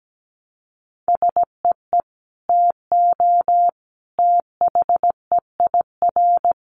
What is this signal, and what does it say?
Morse code, heard as a keyed tone, about 17 wpm, 700 Hz. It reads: SEE TO THEIR
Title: Sentences from Top 1100 Words 17wpm